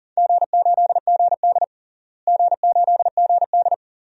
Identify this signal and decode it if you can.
Morse code, heard as a keyed tone, about 40 wpm, 700 Hz. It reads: G8GD G8GD
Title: DXCC-100 Call Signs - 4 Characters - Repeated 3x - 40wpm